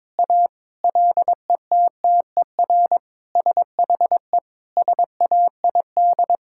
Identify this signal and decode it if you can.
Morse code, heard as a keyed tone, about 22 wpm, 700 Hz. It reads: A LETTER SHE SAID
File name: practice / 22wpm / Sentences with 4-Word Phrases - 22wpm